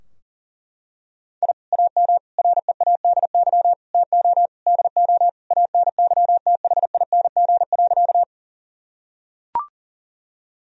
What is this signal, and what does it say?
Morse code, heard as a keyed tone, about 40 wpm, 700 Hz. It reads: I AM READY TO DO ANYTHING.
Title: Sentences from Top 1000 Words - Review Part 2 - 40wpm